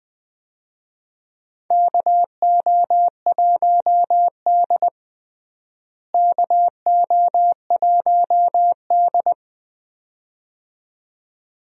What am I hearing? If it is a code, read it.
Morse code, heard as a keyed tone, about 20 wpm, 700 Hz. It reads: KO1D KO1D
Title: USA Call Signs - 4 Characters - Repeated 2x - 20wpm